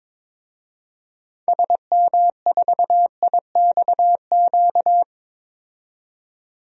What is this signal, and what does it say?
Morse code, heard as a keyed tone, about 22 wpm, 700 Hz. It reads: SM4IXQ
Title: DXCC-100 Call Signs - 6 Characters - 22wpm